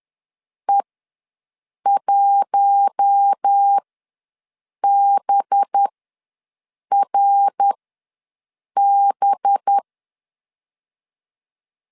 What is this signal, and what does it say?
Morse code, heard as a keyed tone, about 10 wpm, spaced as for 6 wpm, 800 Hz. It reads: E1BRB
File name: LEZIONI / 5 LBVC / 5 LB ws 10wpm 3616